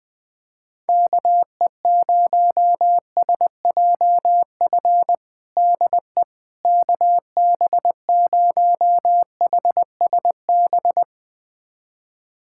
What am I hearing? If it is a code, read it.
Morse code, heard as a keyed tone, about 20 wpm, 700 Hz. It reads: KE0SJF DE KB0HSB
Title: USA Callsign DE Callsign - 6 Characters - 20wpm